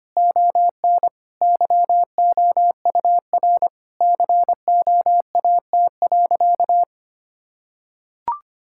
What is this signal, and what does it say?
Morse code, heard as a keyed tone, about 25 wpm, 700 Hz. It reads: ON YOUR COAT.
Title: Sentences from Top 1600 Words 25wpm